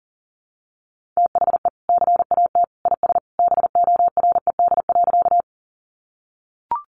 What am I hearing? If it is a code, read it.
Morse code, heard as a keyed tone, about 40 wpm, 700 Hz. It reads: THE CAT IS BORED.
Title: Easy Phrases 40wpm